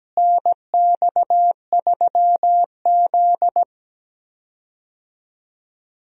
Morse code, heard as a keyed tone, about 17 wpm, 700 Hz. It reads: NX3Z
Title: USA Call Signs - 4 Characters - Repeated 3x - 17wpm